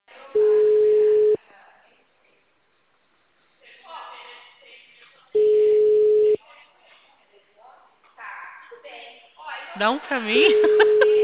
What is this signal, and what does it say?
Ringback tone (Europe, 425 Hz), with speech.
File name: out- -2046- - - .33613